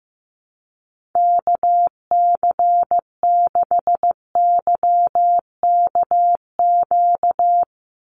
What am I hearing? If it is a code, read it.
Morse code, heard as a keyed tone, about 15 wpm, 700 Hz. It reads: KC6YKQ
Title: USA Call Signs 15wpm